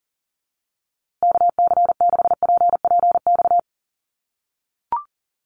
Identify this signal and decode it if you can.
Morse code, heard as a keyed tone, about 40 wpm, 700 Hz. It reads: KC6PPX
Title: USA Call Signs Repeated 3x 40wpm